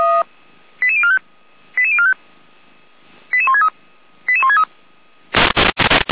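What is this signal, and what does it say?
Five-tone selective call (ZVEI), heard as quick runs of five tones, tones of about 70 ms. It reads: 88045 88045 88163 88163
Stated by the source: ZVEI five-tone selective call received by SDR.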